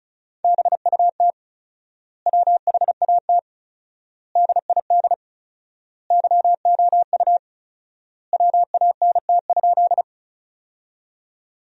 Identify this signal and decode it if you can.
Morse code, heard as a keyed tone, about 34 wpm, 700 Hz. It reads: BUT WHAT DID YOU WANT?
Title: 4x Word-Spacing - Sentences from Top 200 Words - 35wpm